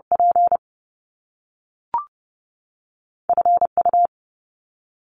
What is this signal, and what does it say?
Morse code, heard as a keyed tone, about 30 wpm, 700 Hz. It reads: P F U
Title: Top Two Letter Combinations in Words 30wpm